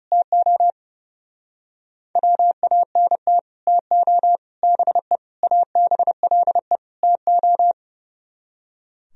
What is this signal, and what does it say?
Morse code, heard as a keyed tone, about 37 wpm, 700 Hz. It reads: TO WANT TO BE ABLE TO
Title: Speed-Racing - Sets of 5 Words 30wpm